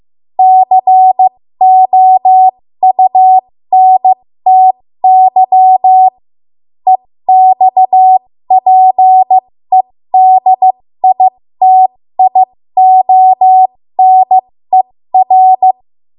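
Morse code, heard as a keyed tone, about 15 wpm, spaced as for 13 wpm, 750 Hz. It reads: COUNTY EXPEDITIONER
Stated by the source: ARRL Morse code practice at 13 WPM.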